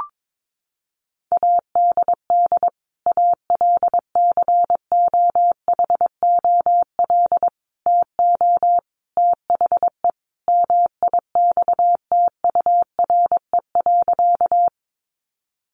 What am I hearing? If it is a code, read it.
Morse code, heard as a keyed tone, about 22 wpm, 700 Hz. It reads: ADD ALCOHOL TO THE MIXTURE.
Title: Sentences from Top 1500 Words 22wpm